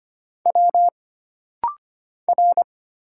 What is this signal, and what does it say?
Morse code, heard as a keyed tone, about 25 wpm, 700 Hz. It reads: W R
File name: Single Character (Letter:Number:Punctuation:Prosign) - Rapid-Fire - 25wpm